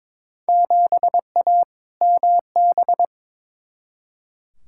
Morse code, heard as a keyed tone, about 22 wpm, 700 Hz. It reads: 7A MB
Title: Field Day Exchanges Encore - 22wpm